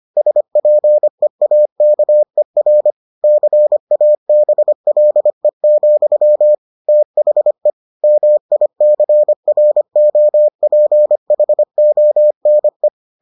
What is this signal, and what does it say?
Morse code, heard as a keyed tone, about 25 wpm, 600 Hz. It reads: SPEAKER CABLE, THE MICROPHONE